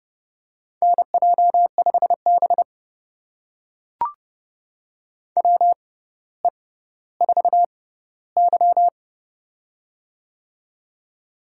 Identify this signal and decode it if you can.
Morse code, heard as a keyed tone, about 31 wpm, spaced as for 16 wpm, 700 Hz. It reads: NJ5B WE4Y
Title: USA Call Signs - 4 Characters - 6x Character Spacing - 30wpm